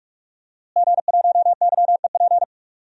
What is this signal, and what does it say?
Morse code, heard as a keyed tone, about 46 wpm, 700 Hz. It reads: G1YEP
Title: DXCC-100 Call Signs - 5 Characters - Repeated 2x - 45wpm